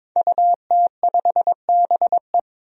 Morse code, heard as a keyed tone, about 22 wpm, 700 Hz. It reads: UT5BE
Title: DXCC-100 Call Signs - 5 Characters - Repeated 2x - 22wpm